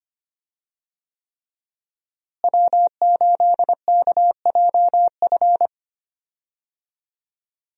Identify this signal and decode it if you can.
Morse code, heard as a keyed tone, about 25 wpm, 700 Hz. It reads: W8KJF
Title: USA Call Signs - 5 Characters - Repeated 3x - 25wpm